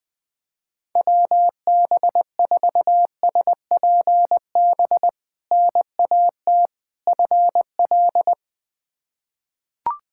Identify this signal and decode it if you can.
Morse code, heard as a keyed tone, about 21 wpm, 700 Hz. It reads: WB4SPB NAT FL
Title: K1USN SST QSO Practice - 20wpm